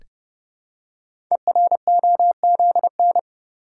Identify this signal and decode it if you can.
Morse code, heard as a keyed tone, about 28 wpm, 700 Hz. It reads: EROZN
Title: Code Groups - 5 Letters - 30wpm